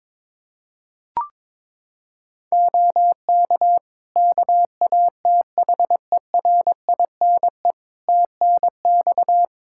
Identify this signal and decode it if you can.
Morse code, heard as a keyed tone, about 22 wpm, 700 Hz. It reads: OK KATHERINE TNX
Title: K1USN SST QSO Practice 22wpm